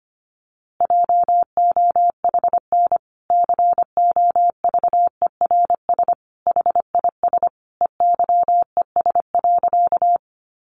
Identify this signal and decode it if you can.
Morse code, heard as a keyed tone, about 25 wpm, 700 Hz. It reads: JOHN COVERS HIS EYES.
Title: Easy Phrases 25wpm